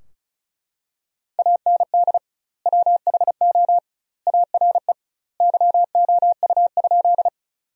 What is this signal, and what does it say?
Morse code, heard as a keyed tone, about 36 wpm, 700 Hz. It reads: AND WHO ARE YOU?
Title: 2x Word-Spacing - Sentences from Top 100 Words - 35wpm